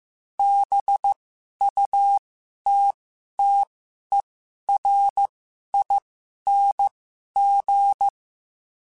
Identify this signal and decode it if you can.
Morse code, heard as a keyed tone, about 15 wpm, spaced as for 11 wpm, 800 Hz. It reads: BUTTERING
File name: Special INGs-mp3s / Special INGs-mp3s / Sngl Ing @10